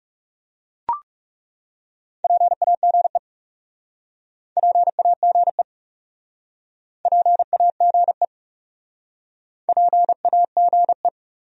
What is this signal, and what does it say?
Morse code, heard as a keyed tone, about 42 wpm, 700 Hz. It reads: PAGE PAGE PAGE PAGE